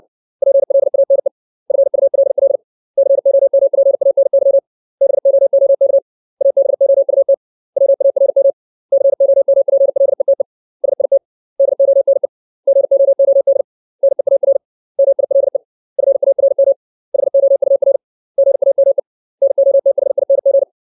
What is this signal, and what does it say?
Morse code, heard as a keyed tone, about 60 wpm, 550 Hz. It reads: PLANE FULL COMPANY BOOK ABOUT WARM COMPLETE SIT DONE GOOD NEAR MILE FARM HOUR GAME NOTHING
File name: 632 words / 632 Words 60WPM